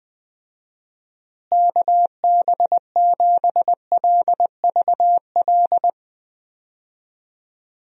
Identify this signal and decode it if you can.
Morse code, heard as a keyed tone, about 20 wpm, 700 Hz. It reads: KB7LVL
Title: USA Call Signs - 6 Characters - Repeated 3x - 20wpm